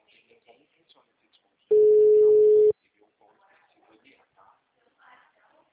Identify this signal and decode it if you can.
Ringback tone (Europe, 425 Hz), with speech.
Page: Sumare - SP